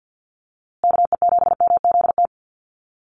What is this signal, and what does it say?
Morse code, heard as a keyed tone, about 48 wpm, 700 Hz. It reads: KE7MZT
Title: USA Call Signs Repeated 2x 50wpm